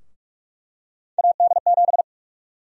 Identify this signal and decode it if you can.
Morse code, heard as a keyed tone, about 46 wpm, 700 Hz. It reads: AD7
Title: USA Call Sign Prefixes - 45wpm